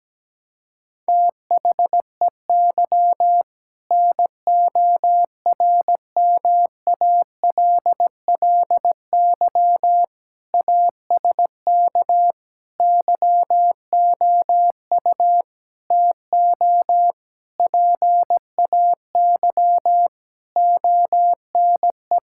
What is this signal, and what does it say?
Morse code, heard as a keyed tone, about 17 wpm, 700 Hz. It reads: THEY NORMALLY ASK YOU TO PAY ONE
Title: Sentences from Top 1000 Words - Review Part 6 - 17wpm